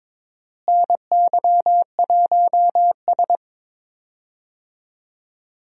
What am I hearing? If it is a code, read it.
Morse code, heard as a keyed tone, about 22 wpm, 700 Hz. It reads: NY1S
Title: USA Call Signs - 4 Characters - Repeated 3x - 22wpm